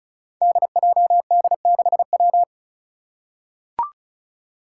Morse code, heard as a keyed tone, about 36 wpm, 700 Hz. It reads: DJD6W
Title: Five Challenging Letter-Number Combinations 35wpm